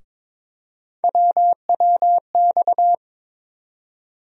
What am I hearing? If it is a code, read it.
Morse code, heard as a keyed tone, about 22 wpm, 700 Hz. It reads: WWX
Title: USA Call Sign Postfixes - 3 Letters 22wpm